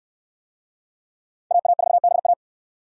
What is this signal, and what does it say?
Morse code, heard as a keyed tone, about 65 wpm, 700 Hz. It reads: AA4BA